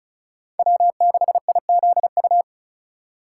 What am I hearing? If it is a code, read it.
Morse code, heard as a keyed tone, about 36 wpm, 700 Hz. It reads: W6IZU
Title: USA Call Signs - 5 Characters - 6x Character Spacing - 35wpm